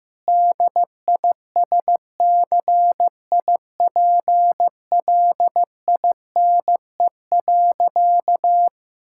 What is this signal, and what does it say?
Morse code, heard as a keyed tone, about 15 wpm, 700 Hz. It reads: DISCIPLINE.
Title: Sentences from Top 1000 Words - Review Part 6 - 15wpm